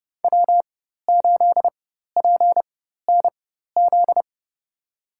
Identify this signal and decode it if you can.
Morse code, heard as a keyed tone, about 30 wpm, spaced as for 14 wpm, 700 Hz. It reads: W8PNZ
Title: USA Call Signs - 5 Characters - 4x Character Spacing - 30wpm